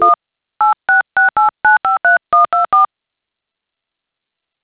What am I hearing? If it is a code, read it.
DTMF, heard as telephone touch tones, tones of about 140 ms. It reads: 1 8668953124
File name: things / adit fxo tellme